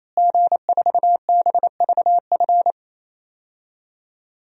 Morse code, heard as a keyed tone, about 27 wpm, 700 Hz. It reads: G4BVF
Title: DXCC-100 Call Signs - 5 Characters - Repeated 3x - 28wpm